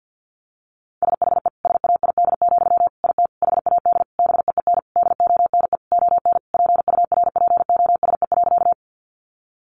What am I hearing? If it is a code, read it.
Morse code, heard as a keyed tone, about 50 wpm, 700 Hz. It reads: SHE SAID, IT HAD BEEN DONE ON PURPOSE.
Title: Sentences from Top 800 Words 50wpm